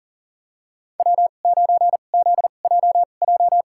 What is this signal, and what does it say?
Morse code, heard as a keyed tone, about 37 wpm, spaced as for 28 wpm, 700 Hz. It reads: W9ZJJ
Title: USA Call Signs - 5 Characters - 2x Character Spacing - 40wpm